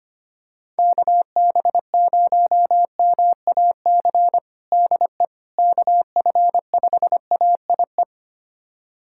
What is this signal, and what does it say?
Morse code, heard as a keyed tone, about 25 wpm, 700 Hz. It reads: KB0MAC DE KF5AIE
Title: USA Callsign DE Callsign - 6 Characters - 25wpm